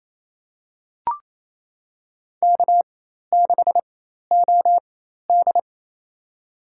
Morse code, heard as a keyed tone, about 28 wpm, spaced as for 13 wpm, 700 Hz. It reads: K6OD
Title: USA Call Signs - 4x Character Spacing - 28wpm